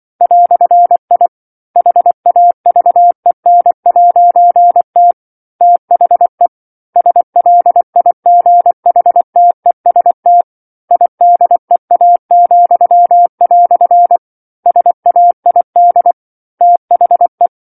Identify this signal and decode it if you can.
Morse code, heard as a keyed tone, about 24 wpm, 700 Hz. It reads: "I HAVEN'T THE SLIGHTEST IDEA," SAID THE
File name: 07-Chapter-07 24 WPM